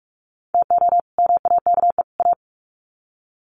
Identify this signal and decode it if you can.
Morse code, heard as a keyed tone, about 44 wpm, 700 Hz. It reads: TO MAKE A
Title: Sets of 3 Words 45wpm